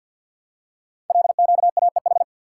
Morse code, heard as a keyed tone, about 51 wpm, 700 Hz. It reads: LQAEH
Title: Code Groups - 5 Letters - 50wpm